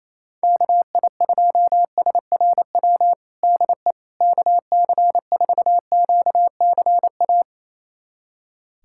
Morse code, heard as a keyed tone, about 28 wpm, 700 Hz. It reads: KI2SRW DE KC4QCA
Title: USA Callsign DE Callsign 28wpm